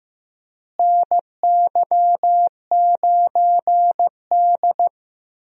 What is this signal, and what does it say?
Morse code, heard as a keyed tone, about 15 wpm, 700 Hz. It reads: NY9D
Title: USA Call Signs - 4 Characters - Repeated 2x - 15wpm